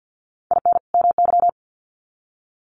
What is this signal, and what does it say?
Morse code, heard as a keyed tone, about 50 wpm, 700 Hz. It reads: IN MY
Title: Sets of 2 Words 50wpm